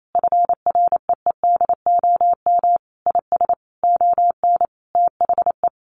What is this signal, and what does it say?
Morse code, heard as a keyed tone, about 28 wpm, 700 Hz. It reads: FREEDOM IS ON THE
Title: Sentences from Top 1300 Words 28wpm